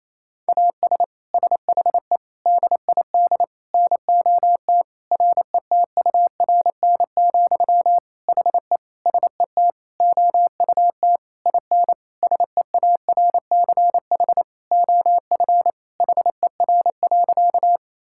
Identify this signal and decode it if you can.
Morse code, heard as a keyed tone, about 28 wpm, 700 Hz. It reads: AS SHE DID NOT RETURN, HE SET OUT IN SEARCH OF HER.
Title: Sentences from Top 1000 Words - Review Part 5 - 28wpm